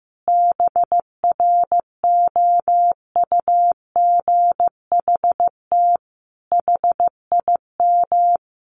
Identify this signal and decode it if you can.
Morse code, heard as a keyed tone, about 15 wpm, 700 Hz. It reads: BROUGHT HIM
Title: Sentences from Top 1100 Words 15wpm